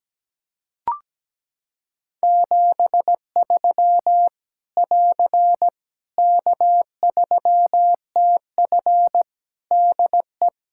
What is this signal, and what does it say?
Morse code, heard as a keyed tone, about 17 wpm, 700 Hz. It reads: 73 <AR> K3TF DE
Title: Practice QSO - Exchange 4 - Ending - 17wpm